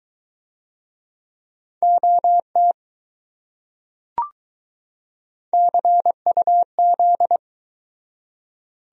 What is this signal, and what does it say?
Morse code, heard as a keyed tone, about 20 wpm, 700 Hz. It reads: OT CUZ